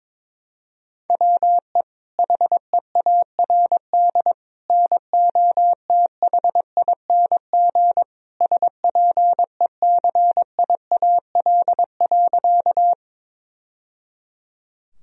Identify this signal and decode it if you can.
Morse code, heard as a keyed tone, about 22 wpm, 700 Hz. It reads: WE HEARD NOTHING SPECIAL.
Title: Sentences from Top 1000 Words - Review Part 3 - 22wpm